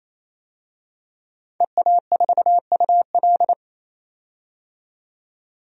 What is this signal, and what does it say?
Morse code, heard as a keyed tone, about 28 wpm, 700 Hz. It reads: EA4UL
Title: DXCC-100 Call Signs - 5 Characters - 28wpm